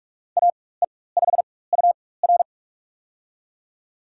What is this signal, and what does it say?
Morse code, heard as a keyed tone, about 42 wpm, spaced as for 21 wpm, 700 Hz. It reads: AE5UR
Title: USA Call Signs - 5 Characters - 4x Character Spacing - 45wpm